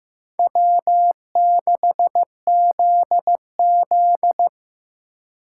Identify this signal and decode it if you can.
Morse code, heard as a keyed tone, about 18 wpm, 700 Hz. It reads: W6ZZ
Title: Speed Racing - USA 1x2 Call Signs 15wpm